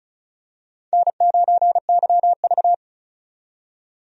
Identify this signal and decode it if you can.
Morse code, heard as a keyed tone, about 34 wpm, 700 Hz. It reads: N9YV
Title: USA Call Signs - 4 Characters - 35wpm